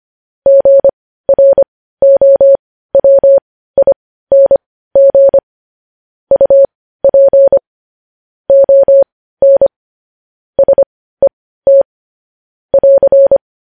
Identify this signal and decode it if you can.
Morse code, heard as a keyed tone, about 25 wpm, spaced as for 15 wpm, 550 Hz. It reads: GROWING UP ON SET <AR>
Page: news headlines in morse code at 15 wpm